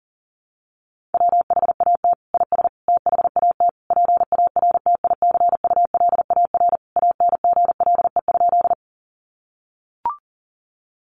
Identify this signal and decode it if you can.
Morse code, heard as a keyed tone, about 40 wpm, 700 Hz. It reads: WHAT IS THAT PARTICULAR ANGLE?
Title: Sentences from Top 1500 Words 40wpm